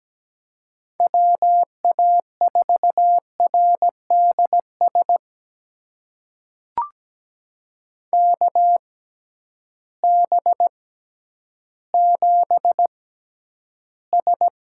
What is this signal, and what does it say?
Morse code, heard as a keyed tone, about 17 wpm, spaced as for 10 wpm, 700 Hz. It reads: WA4RDS KB7S
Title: USA Call Signs - 6 Characters - 6x Character Spacing - 17wpm